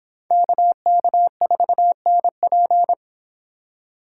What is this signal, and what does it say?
Morse code, heard as a keyed tone, about 29 wpm, 700 Hz. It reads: KK4NP